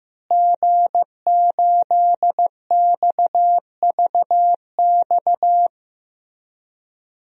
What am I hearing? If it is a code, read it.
Morse code, heard as a keyed tone, about 15 wpm, 700 Hz. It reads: G8XVX
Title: DXCC-100 Call Signs - 5 Characters - Repeated 2x - 15wpm